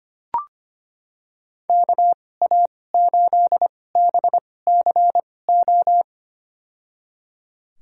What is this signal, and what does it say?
Morse code, heard as a keyed tone, about 25 wpm, spaced as for 18 wpm, 700 Hz. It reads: KA8BCO
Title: USA Call Signs - 6 Characters - 2x Character Spacing - 25wpm